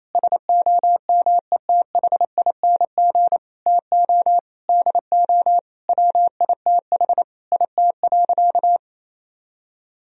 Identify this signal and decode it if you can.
Morse code, heard as a keyed tone, about 28 wpm, 700 Hz. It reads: SOMETHING TO DO WITH IT.
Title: Sentences from Top 1000 Words - Review Part 2 - 28wpm